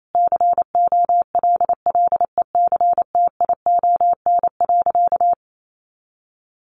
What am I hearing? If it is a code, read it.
Morse code, heard as a keyed tone, about 28 wpm, 700 Hz. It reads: COLLECTION.
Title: Sentences from Top 1200 Words 28wpm